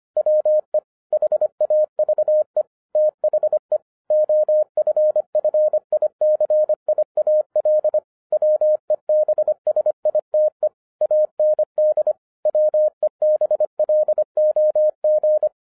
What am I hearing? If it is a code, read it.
Morse code, heard as a keyed tone, about 25 wpm, 610 Hz. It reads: WE HAVE THE OFFICIAL WEBSITE AND WEBLOG